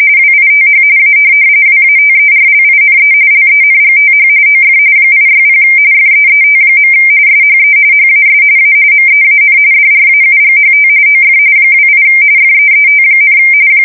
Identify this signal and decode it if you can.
RTTY, about 45.45 baud. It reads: WITH A WATERLINE LENGTH OF 860 FEET. SHE HAS A MEAN DRAUGHT OF 28 FEET,
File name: fsk170 / USS-New-Jersey